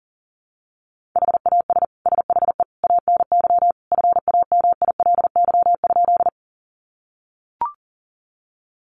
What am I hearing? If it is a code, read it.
Morse code, heard as a keyed tone, about 40 wpm, 700 Hz. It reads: HAS SHE ANY FAMILY?
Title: Sentences from Top 400 Words 40wpm